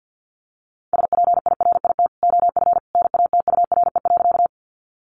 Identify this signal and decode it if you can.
Morse code, heard as a keyed tone, about 50 wpm, 700 Hz. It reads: SPIRIT OF NATURE.
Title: Sentences from Top 1400 Words 50wpm